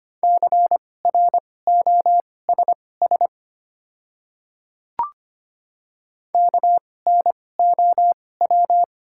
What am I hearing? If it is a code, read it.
Morse code, heard as a keyed tone, about 25 wpm, spaced as for 18 wpm, 700 Hz. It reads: CROSS KNOW
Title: 2x Character Spacing - Top 300 Words - 25wpm